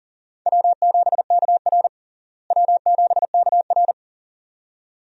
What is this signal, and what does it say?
Morse code, heard as a keyed tone, about 37 wpm, 700 Hz. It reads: W7KR W7KR
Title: USA Call Signs - 4 Characters - Repeated 2x - 40wpm